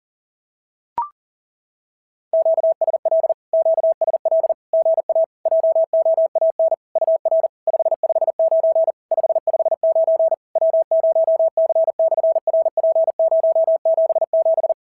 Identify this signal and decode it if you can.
Morse code, heard as a keyed tone, about 37 wpm, 650 Hz. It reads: QSL QSL GA JOAN UR 559 559 W0C/RP077
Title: SOTA QSO Practice 40wpm